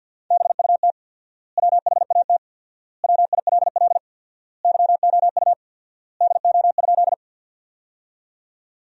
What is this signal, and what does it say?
Morse code, heard as a keyed tone, about 53 wpm, 700 Hz. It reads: BUT WHAT WILL YOU DO?
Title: 4x Word-Spacing - Sentences from Top 100 Words - 50wpm